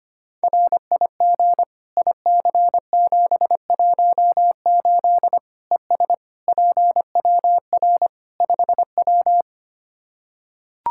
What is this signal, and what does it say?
Morse code, heard as a keyed tone, about 25 wpm, 700 Hz. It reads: RIG IC718 ES PWR 5W
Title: Practice QSO - Exchange 2 - Info - 25wpm